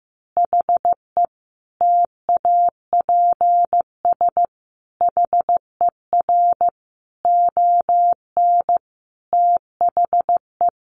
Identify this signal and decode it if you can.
Morse code, heard as a keyed tone, about 15 wpm, 700 Hz. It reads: HE TAPS HER ON THE
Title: Additional Easy Phrases 15wpm